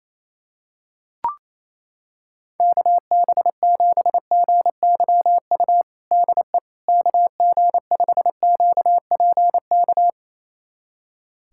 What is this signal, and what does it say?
Morse code, heard as a keyed tone, about 28 wpm, 700 Hz. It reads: KB7GYU DE KG5QPK
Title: USA Callsign DE Callsign - 6 Characters - 28wpm